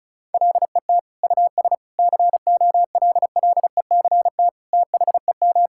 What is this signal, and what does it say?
Morse code, heard as a keyed tone, about 36 wpm, 700 Hz. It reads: LET US COLLECT THEM
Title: Sentences from Top 1600 Words 35wpm